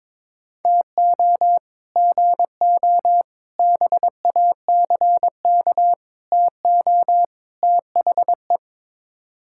Morse code, heard as a keyed tone, about 22 wpm, 700 Hz. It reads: TO GO BACK TO THE
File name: practice / 22wpm / Build A Phrase - 5-Word Phrases - 22wpm